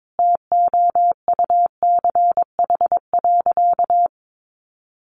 Morse code, heard as a keyed tone, about 22 wpm, 700 Hz. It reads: TOUCH.
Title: Sentences from Top 1500 Words 22wpm